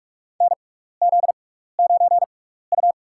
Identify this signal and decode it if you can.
Morse code, heard as a keyed tone, about 46 wpm, spaced as for 15 wpm, 700 Hz. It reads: NZ9U
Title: USA Call Signs - 4 Characters - 6x Character Spacing - 45wpm